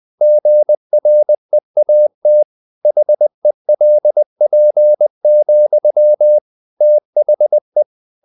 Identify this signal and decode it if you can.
Morse code, heard as a keyed tone, about 20 wpm, 600 Hz. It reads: GREAT HELP, THE
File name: the old man and the sea / the old man and the sea part 1 20wpm